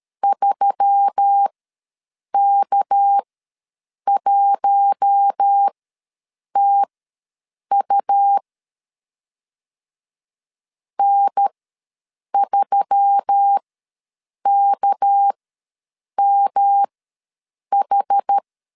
Morse code, heard as a keyed tone, about 13 wpm, spaced as for 7 wpm, 800 Hz. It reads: 3K1TU N3KMH
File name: LEZIONI / 6 PKX0 / 6 PK ws 12wpm 3116